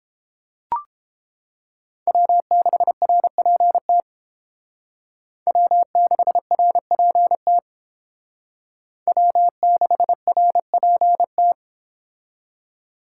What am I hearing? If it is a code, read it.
Morse code, heard as a keyed tone, about 30 wpm, 700 Hz. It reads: W6RPT W6RPT W6RPT